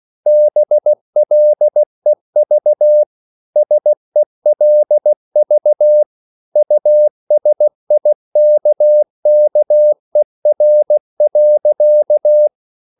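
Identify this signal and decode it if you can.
Morse code, heard as a keyed tone, about 16 wpm, 600 Hz. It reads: BLEV SELV USIKKER.
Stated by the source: Slow (QRS) Morse code at 16 wpm, from CW med Gnister.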